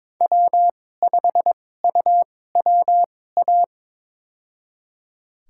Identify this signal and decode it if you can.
Morse code, heard as a keyed tone, about 22 wpm, spaced as for 16 wpm, 700 Hz. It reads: W5UWA
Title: USA Call Signs - 2x Character Spacing - 22wpm